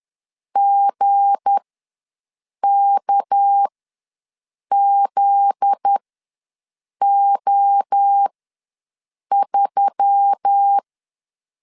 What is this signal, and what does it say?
Morse code, heard as a keyed tone, about 10 wpm, spaced as for 6 wpm, 800 Hz. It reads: GKZO3
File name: LEZIONI / 9 28 ws 10wpm 4124